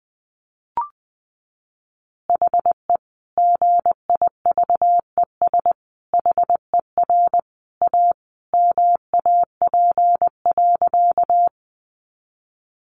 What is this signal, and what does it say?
Morse code, heard as a keyed tone, about 20 wpm, 700 Hz. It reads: HE GIVES HER A MAP.
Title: Additional Easy Phrases 20wpm